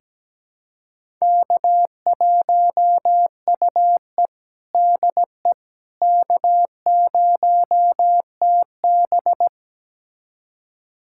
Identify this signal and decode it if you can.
Morse code, heard as a keyed tone, about 17 wpm, 700 Hz. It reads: K1UE DE K0TB
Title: USA Callsign DE Callsign - 4 Characters - 17wpm